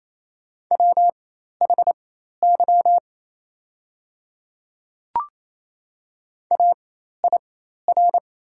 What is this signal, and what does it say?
Morse code, heard as a keyed tone, about 27 wpm, spaced as for 13 wpm, 700 Hz. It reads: WHY AIR
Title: 4x Character Spacing - Top 200 Words - 28wpm